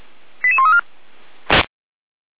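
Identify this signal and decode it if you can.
Five-tone selective call (ZVEI), heard as quick runs of five tones, tones of about 70 ms. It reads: 88134
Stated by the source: Received by SDR, ZVEI five-tone call from the feuerwehr-orlamuende.org (Orlamünde fire brigade) audio server.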